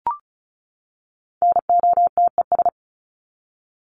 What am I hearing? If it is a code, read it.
Morse code, heard as a keyed tone, about 34 wpm, 700 Hz. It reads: NOTES
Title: ES:S Suffix - Top 100 Words with Base Words 35wpm